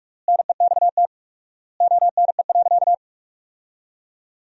Morse code, heard as a keyed tone, about 41 wpm, 700 Hz. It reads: NEXT ONE.
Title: 4x Word-Spacing - Sentences from Top 400 Words - 45wpm